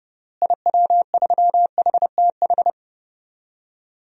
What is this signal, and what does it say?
Morse code, heard as a keyed tone, about 30 wpm, 700 Hz. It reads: IW3HTH
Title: DXCC-100 Call Signs - 6 Characters - Repeated 3x - 30wpm